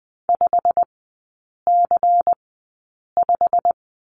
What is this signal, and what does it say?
Morse code, heard as a keyed tone, about 20 wpm, spaced as for 8 wpm, 700 Hz. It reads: 5C5
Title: 133 Lesson 22e - Question Mark - Three Characters 20wpm